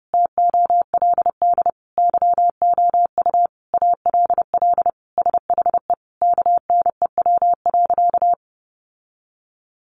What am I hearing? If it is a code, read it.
Morse code, heard as a keyed tone, about 30 wpm, 700 Hz. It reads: TOLD YOU ALL SHE KNEW.
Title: Sentences from Top 1300 Words 30wpm